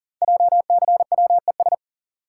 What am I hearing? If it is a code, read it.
Morse code, heard as a keyed tone, about 41 wpm, 690 Hz. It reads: JCWES
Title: Code Groups - 5 Letters - 40wpm